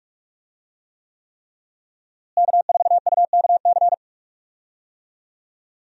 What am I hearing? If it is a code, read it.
Morse code, heard as a keyed tone, about 46 wpm, 700 Hz. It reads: K4UKC
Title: USA Call Signs - 5 Characters - Repeated 3x - 45wpm